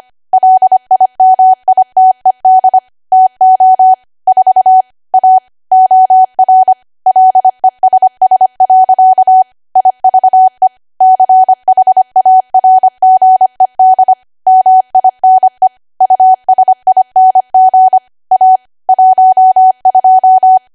Morse code, heard as a keyed tone, about 25 wpm, 750 Hz. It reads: LIMITED TO 4 A OR LESS. IVE CHARGED MINE USING A 12